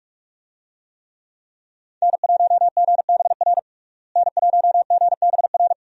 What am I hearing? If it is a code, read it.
Morse code, heard as a keyed tone, about 46 wpm, 700 Hz. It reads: N1GBR N1GBR
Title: USA Call Signs - 5 Characters - Repeated 3x - 45wpm